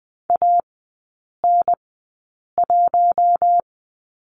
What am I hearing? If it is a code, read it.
Morse code, heard as a keyed tone, about 20 wpm, spaced as for 8 wpm, 700 Hz. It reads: AN1
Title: 041 Lesson 8e - Number 1 - Three Characters 20wpm